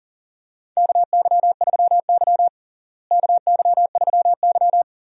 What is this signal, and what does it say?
Morse code, heard as a keyed tone, about 38 wpm, 700 Hz. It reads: KY3Y KY3Y
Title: USA Call Signs - 4 Characters - Repeated 2x - 40wpm